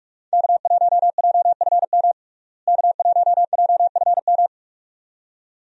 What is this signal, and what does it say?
Morse code, heard as a keyed tone, about 46 wpm, 700 Hz. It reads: K1JFM K1JFM
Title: USA Call Signs - 5 Characters - Repeated 2x - 45wpm